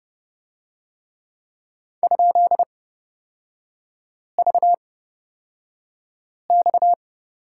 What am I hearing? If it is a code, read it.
Morse code, heard as a keyed tone, about 31 wpm, spaced as for 5 wpm, 690 Hz. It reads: ?VX
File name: Lesson 37e - Letter x - Three Characters 30wpm